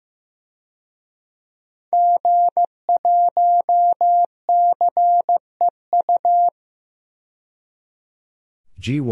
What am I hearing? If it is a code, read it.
Morse code, heard as a keyed tone, about 15 wpm, 700 Hz. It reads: G1CEU
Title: DXCC-100 Call Signs - 5 Characters - 15wpm